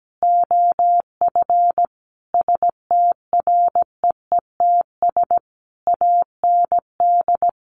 Morse code, heard as a keyed tone, about 17 wpm, 700 Hz. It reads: OF STREETS AND
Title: A Sound Of Thunder 17wpm